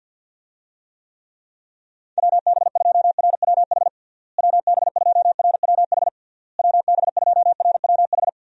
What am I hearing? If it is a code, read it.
Morse code, heard as a keyed tone, about 52 wpm, 700 Hz. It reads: WB2RWH WB2RWH WB2RWH
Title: USA Call Signs - 6 Characters - Repeated 3x - 50wpm